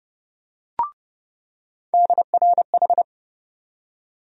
Morse code, heard as a keyed tone, about 30 wpm, 700 Hz. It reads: DRH
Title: USA Call Sign Postfixes - 3 Letters 30wpm